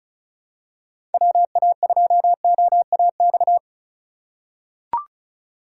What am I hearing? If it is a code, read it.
Morse code, heard as a keyed tone, about 36 wpm, 700 Hz. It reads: WA2OAX
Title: USA Call Signs - 6 Characters - 2x Character Spacing - 35wpm